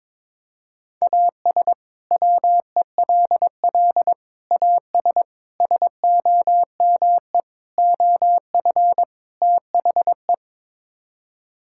Morse code, heard as a keyed tone, about 22 wpm, 700 Hz. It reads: AS WELL AS SOME OF THE
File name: practice / 22wpm / Sets of 6 Words - 22wpm